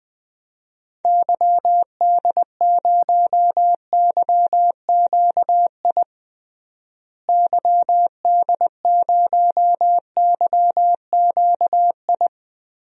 Morse code, heard as a keyed tone, about 20 wpm, 700 Hz. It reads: YD0YQI YD0YQI
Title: DXCC-100 Call Signs - 6 Characters - Repeated 2x - 20wpm